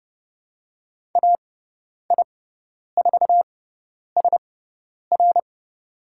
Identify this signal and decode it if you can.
Morse code, heard as a keyed tone, about 31 wpm, spaced as for 10 wpm, 700 Hz. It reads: AI4SR
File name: Lesson 17g - Letter u - Call Signs 30wpm